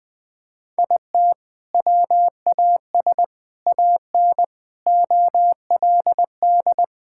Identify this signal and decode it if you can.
Morse code, heard as a keyed tone, about 21 wpm, 700 Hz. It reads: IT WAS AN OLD
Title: Sentences from Top 1000 Words - Review Part 5 - 20wpm